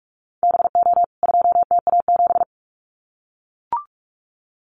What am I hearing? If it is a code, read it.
Morse code, heard as a keyed tone, about 44 wpm, 700 Hz. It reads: BO 2TA7
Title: CWT Exchange 45wpm